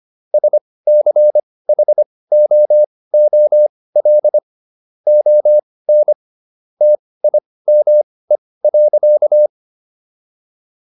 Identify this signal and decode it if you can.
Morse code, heard as a keyed tone, about 25 wpm, spaced as for 18 wpm, 600 Hz. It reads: SCHOOL ON TIME.